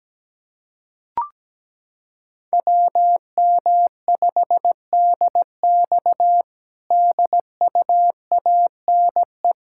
Morse code, heard as a keyed tone, about 17 wpm, 700 Hz. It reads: WM5DX DUANE
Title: K1USN SST QSO Practice - 17wpm